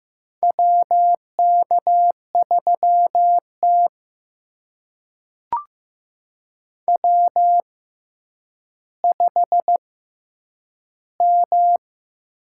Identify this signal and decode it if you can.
Morse code, heard as a keyed tone, about 15 wpm, spaced as for 9 wpm, 700 Hz. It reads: WK3T W5M
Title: USA Call Signs - 4 Characters - 6x Character Spacing - 15wpm